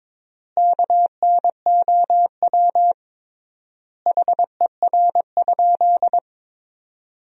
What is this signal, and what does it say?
Morse code, heard as a keyed tone, about 22 wpm, 700 Hz. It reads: KNOW HER?
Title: 3x Word-Spacing - Sentences from Top 100 Words - 22wpm